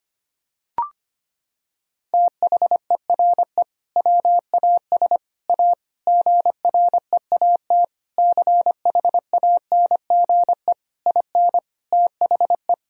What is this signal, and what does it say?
Morse code, heard as a keyed tone, about 25 wpm, 700 Hz. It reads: THERE WAS A GREAT CHANGE IN THE
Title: Sentences from Top 1000 Words - Review Part 3 - 25wpm